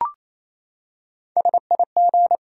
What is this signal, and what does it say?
Morse code, heard as a keyed tone, about 28 wpm, 700 Hz. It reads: SIG
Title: Top 90 QSO Elements 28wpm